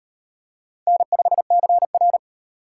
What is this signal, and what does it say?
Morse code, heard as a keyed tone, about 34 wpm, 700 Hz. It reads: N5CR